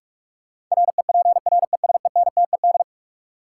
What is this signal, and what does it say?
Morse code, heard as a keyed tone, about 46 wpm, 700 Hz. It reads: REPRESENTED
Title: Sets of 5 Words - New Word Drill - 45wpm